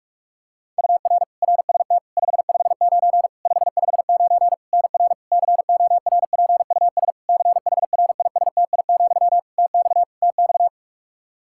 Morse code, heard as a keyed tone, about 46 wpm, 700 Hz. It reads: UR RST 559 559 NR CORPUS CHRISTI, TX TX
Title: K1USN SST QSO Practice 45wpm